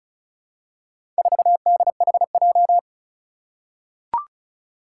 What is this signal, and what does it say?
Morse code, heard as a keyed tone, about 36 wpm, 700 Hz. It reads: 4DHJ
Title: Four Challenging Letter-Number Combinations 35wpm